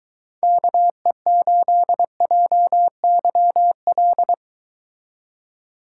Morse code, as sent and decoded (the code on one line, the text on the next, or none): -.- . ---.. .--- -.-- .-..
KE8JYL